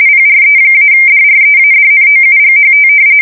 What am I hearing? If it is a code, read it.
RTTY, about 45.45 baud. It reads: PRE-DREADNOUGHT
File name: fsk170 / battleships